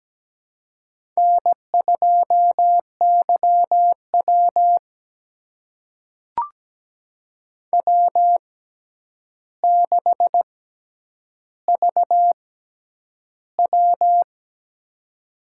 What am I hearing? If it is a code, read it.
Morse code, heard as a keyed tone, about 17 wpm, spaced as for 9 wpm, 700 Hz. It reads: N2YW W6VW
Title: USA Call Signs - 4 Characters - 6x Character Spacing - 17wpm